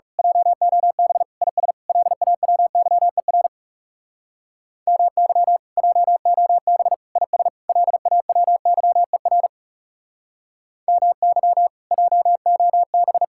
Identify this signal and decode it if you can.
Morse code, heard as a keyed tone, about 34 wpm, 700 Hz. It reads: JOB IS LAWYER MY JOB IS LAWYER MY JOB
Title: Speed-Racing Common QSO Phrases Encore - 30wpm